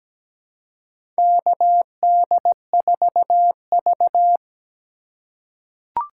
Morse code, heard as a keyed tone, about 17 wpm, 700 Hz. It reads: KD4V
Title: USA Call Signs - 4 Characters - 4x Character Spacing - 17wpm